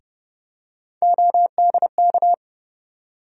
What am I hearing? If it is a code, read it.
Morse code, heard as a keyed tone, about 31 wpm, 700 Hz. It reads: ODK
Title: USA Call Sign Postfixes - 3 Letters 30wpm